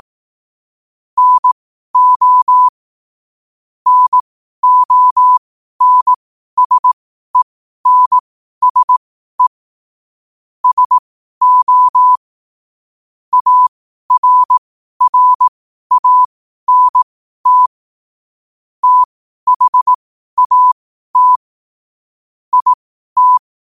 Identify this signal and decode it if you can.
Morse code, heard as a keyed tone, about 18 wpm, spaced as for 12 wpm, 1000 Hz. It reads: NO NONSENSE SO ARRANT THAT IT
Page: New quotes every day in morse code at 12 Words per minute.